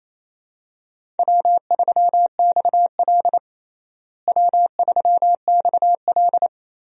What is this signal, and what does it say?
Morse code, heard as a keyed tone, about 28 wpm, 700 Hz. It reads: W3XL W3XL
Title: USA Call Signs - 4 Characters - Repeated 3x - 28wpm